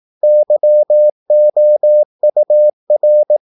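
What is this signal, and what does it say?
Morse code, heard as a keyed tone, about 18 wpm, 600 Hz. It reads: YOUR
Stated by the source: Morse practice at 18 wpm, English text, from CW med Gnister.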